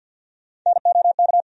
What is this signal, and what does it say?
Morse code, heard as a keyed tone, about 47 wpm, 700 Hz. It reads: NOK
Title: USA Call Sign Postfixes - 3 Letters 50wpm